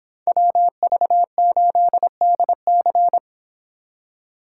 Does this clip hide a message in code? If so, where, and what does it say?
Yes: Morse code from 0.3 to 3.2 s, heard as a keyed tone, about 25 wpm, 700 Hz. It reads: WV8DC